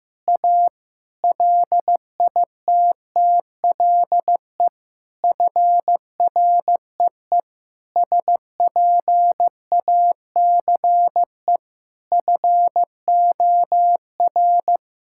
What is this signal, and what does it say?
Morse code, heard as a keyed tone, about 15 wpm, 700 Hz. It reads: A LITTLE FREE SPACE FOR
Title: Sentences from Top 1000 Words - Review Part 5 - 15wpm